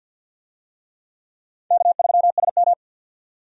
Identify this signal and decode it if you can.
Morse code, heard as a keyed tone, about 53 wpm, 700 Hz. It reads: K3SM
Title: USA Call Signs - 4 Characters - 50wpm